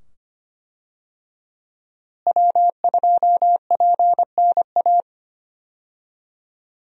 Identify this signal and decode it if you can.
Morse code, heard as a keyed tone, about 25 wpm, 700 Hz. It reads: W2PNA
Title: USA Call Signs - 5 Characters - Repeated 2x - 25wpm